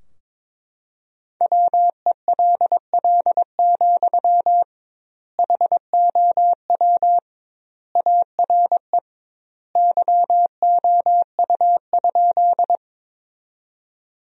Morse code, heard as a keyed tone, about 22 wpm, 700 Hz. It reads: WELL, HOW ARE YOU?
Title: 2x Word-Spacing - Sentences from Top 200 Words - 22wpm